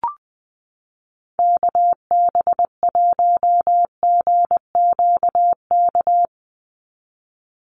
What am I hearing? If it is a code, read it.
Morse code, heard as a keyed tone, about 20 wpm, 700 Hz. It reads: KB1GQK
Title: USA Call Signs 20wpm